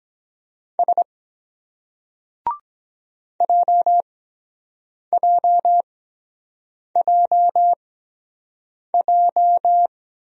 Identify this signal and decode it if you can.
Morse code, heard as a keyed tone, about 23 wpm, spaced as for 7 wpm, 700 Hz. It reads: S JJJJ